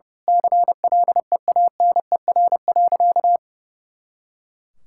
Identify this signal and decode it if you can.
Morse code, heard as a keyed tone, about 30 wpm, 700 Hz. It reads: CLEANER.
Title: Pangram Challenge v2 - 30wpm